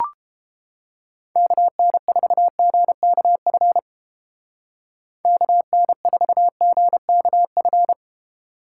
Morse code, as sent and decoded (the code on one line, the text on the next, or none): -.- -. ....- --. -.- ..-. / -.- -. ....- --. -.- ..-.
KN4GKF KN4GKF